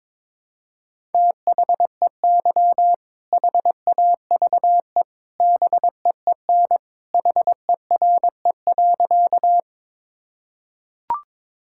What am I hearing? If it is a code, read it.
Morse code, heard as a keyed tone, about 22 wpm, 700 Hz. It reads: THEY HAVE BEEN HERE.
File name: practice / 22wpm / Sentences with 2-Word Phrases - 22wpm